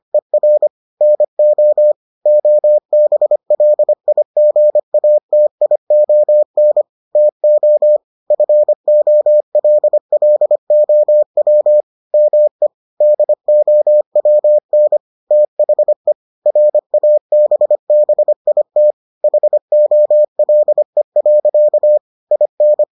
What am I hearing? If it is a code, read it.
Morse code, heard as a keyed tone, about 25 wpm, 600 Hz. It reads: ER NO OBLIGATION TO FOLLOW ME DOWN THE RABBIT HOLE. IN